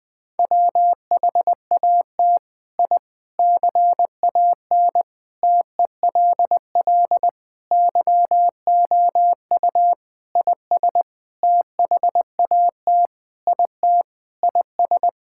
Morse code, heard as a keyed tone, about 20 wpm, 700 Hz. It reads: WHAT I CAN TELL YOU IS THAT IT IS
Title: Sentences with 7-Word Phrases - 20wpm